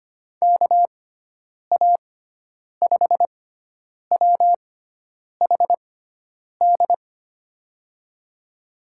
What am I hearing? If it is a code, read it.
Morse code, heard as a keyed tone, about 25 wpm, spaced as for 9 wpm, 700 Hz. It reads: KA5WHD
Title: USA Call Signs - 6 Characters - 6x Character Spacing - 25wpm